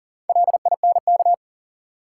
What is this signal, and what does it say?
Morse code, heard as a keyed tone, about 40 wpm, 700 Hz. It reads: LINK
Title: Sentences from Top 1000 Words - Review Part 7 - New Word Drill - 40wpm